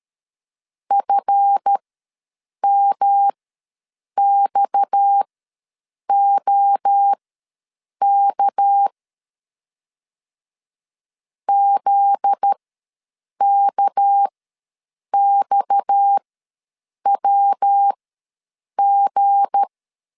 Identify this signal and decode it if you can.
Morse code, heard as a keyed tone, about 13 wpm, spaced as for 7 wpm, 800 Hz. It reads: FMXOK ZKXWG
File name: LEZIONI / 7 GF9Z / 7 GF9Z ws 12wpm 3321